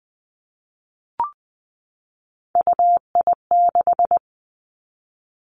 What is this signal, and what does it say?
Morse code, heard as a keyed tone, about 20 wpm, 700 Hz. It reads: UI6
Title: Three Challenging Letter-Number Combinations 20wpm